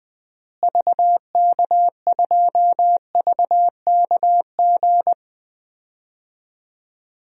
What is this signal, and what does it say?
Morse code, heard as a keyed tone, about 20 wpm, 700 Hz. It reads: VK2VKG
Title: DXCC-100 Call Signs - 6 Characters - Repeated 2x - 20wpm